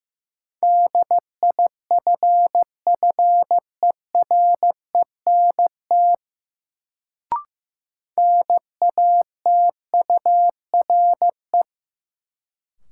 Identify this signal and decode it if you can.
Morse code, heard as a keyed tone, about 16 wpm, 700 Hz. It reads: DIFFERENT NATURE
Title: Sentences from Top 600 Words - New Word Drill - 15wpm